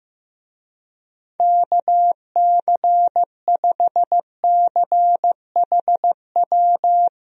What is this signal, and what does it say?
Morse code, heard as a keyed tone, about 15 wpm, 700 Hz. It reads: KC5CHW
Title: USA Call Signs - 6 Characters - Repeated 3x - 15wpm